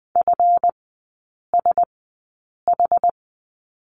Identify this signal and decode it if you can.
Morse code, heard as a keyed tone, about 20 wpm, spaced as for 8 wpm, 700 Hz. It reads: FSH
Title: 178 Lesson 29e - Slash - Three Characters 20wpm